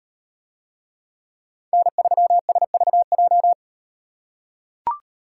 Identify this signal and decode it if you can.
Morse code, heard as a keyed tone, about 38 wpm, 700 Hz. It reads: N3SVJ